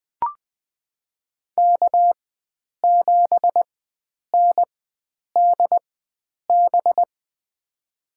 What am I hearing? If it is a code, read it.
Morse code, heard as a keyed tone, about 20 wpm, spaced as for 9 wpm, 700 Hz. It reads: K7NDB
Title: USA Call Signs - 5 Characters - 4x Character Spacing - 20wpm